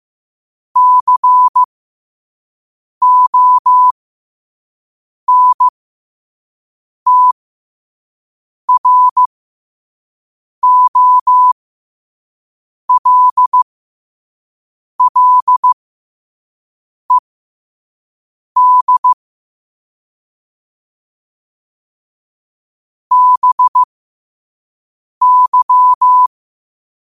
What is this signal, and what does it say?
Morse code, heard as a keyed tone, about 15 wpm, spaced as for 5 wpm, 1000 Hz. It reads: CONTROLLED BY
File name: qotdfiles / 5wpm